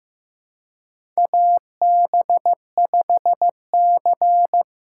Morse code, heard as a keyed tone, about 15 wpm, 700 Hz. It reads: AB5C
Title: K1USN SST QSO Practice - 15wpm